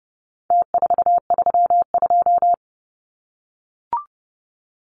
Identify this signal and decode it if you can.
Morse code, heard as a keyed tone, about 30 wpm, 700 Hz. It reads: T432
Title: Numbers - Contest Style - 4-Digit 30wpm